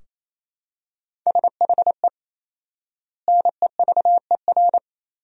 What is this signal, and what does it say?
Morse code, heard as a keyed tone, about 28 wpm, 700 Hz. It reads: SHE NEVER
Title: 4x Word-Spacing - Sentences from Top 300 Words - 28wpm